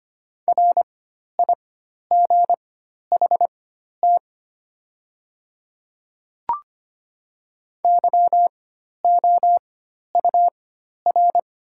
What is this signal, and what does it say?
Morse code, heard as a keyed tone, about 25 wpm, spaced as for 12 wpm, 700 Hz. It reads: RIGHT YOUR
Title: 4x Character Spacing - Top 200 Words - 25wpm